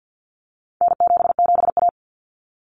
Morse code, heard as a keyed tone, about 50 wpm, 700 Hz. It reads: N77A
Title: Common Cut Numbers - 4-Digit 50wpm